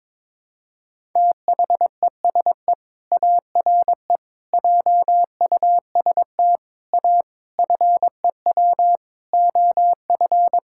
Morse code, heard as a keyed tone, about 22 wpm, 700 Hz. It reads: THESE ARE JUST A FEW OF
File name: practice / 22wpm / Sets of 7 Words - 22wpm